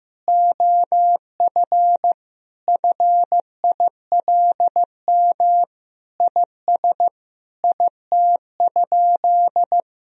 Morse code, heard as a keyed tone, about 15 wpm, 700 Hz. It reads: OF FILM IS IT?
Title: Sentences from Top 1000 Words - Review Part 3 - 15wpm